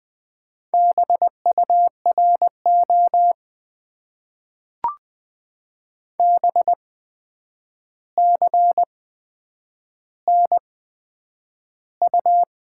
Morse code, heard as a keyed tone, about 20 wpm, spaced as for 9 wpm, 700 Hz. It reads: BURO BCNU
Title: Top 90 QSO Elements - 8x Character Spacing - 20wpm